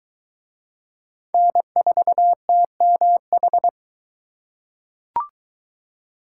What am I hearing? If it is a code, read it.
Morse code, heard as a keyed tone, about 23 wpm, 700 Hz. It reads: N4TMH